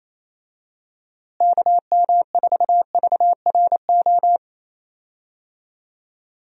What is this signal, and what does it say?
Morse code, heard as a keyed tone, about 28 wpm, 700 Hz. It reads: KM4VRO
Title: USA Call Signs - 6 Characters - Repeated 3x - 28wpm